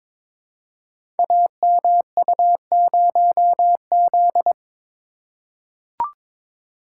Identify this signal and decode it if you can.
Morse code, heard as a keyed tone, about 22 wpm, 700 Hz. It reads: AMU0Z
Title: Code Groups - 5 Letter:Numbers - 22wpm